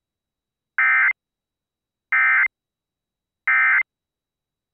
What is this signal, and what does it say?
Emergency Alert System (SAME) tones, heard as harsh buzzing bursts. It is heard from 0.8 to 3.8 s.